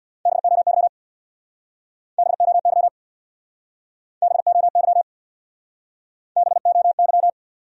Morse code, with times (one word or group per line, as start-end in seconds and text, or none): 0.3-0.9 BOY
2.2-2.9 BOY
4.2-5.0 BOY
6.4-7.3 BOY